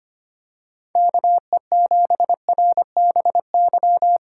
Morse code, heard as a keyed tone, about 25 wpm, 700 Hz. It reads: KE7RBY
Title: USA Call Signs - 4x Character Spacing - 25wpm